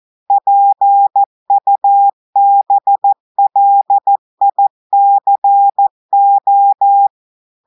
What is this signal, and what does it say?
Morse code, heard as a keyed tone, about 14 wpm, 800 Hz. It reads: PUBLICO
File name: audios / portugais / portugais texte en clair 2 vitesse 14 mots minutes